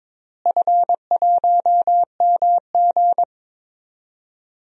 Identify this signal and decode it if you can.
Morse code, heard as a keyed tone, about 22 wpm, 700 Hz. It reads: F1MG
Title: DXCC-100 Call Signs - 4 Characters - Repeated 3x - 22wpm